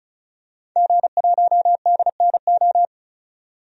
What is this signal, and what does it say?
Morse code, heard as a keyed tone, about 34 wpm, 700 Hz. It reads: G1DNO
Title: DXCC-100 Call Signs - 5 Characters - 35wpm